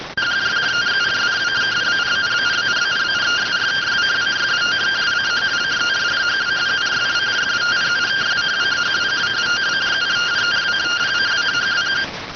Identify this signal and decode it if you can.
Two-tone teleprinter signal, about 170 Hz shift.
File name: findings / S ss-portrait.gen-01 rx